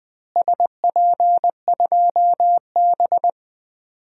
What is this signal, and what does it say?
Morse code, heard as a keyed tone, about 20 wpm, 700 Hz. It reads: SP2B
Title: DXCC-100 Call Signs - 4 Characters - Repeated 2x - 20wpm